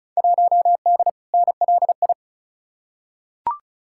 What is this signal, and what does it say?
Morse code, heard as a keyed tone, about 36 wpm, 700 Hz. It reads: 1D NLI
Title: Field Day Exchanges - 35wpm